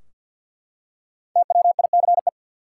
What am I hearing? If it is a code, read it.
Morse code, heard as a keyed tone, about 51 wpm, 700 Hz. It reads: TWICE
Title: Five Letter Words - 50wpm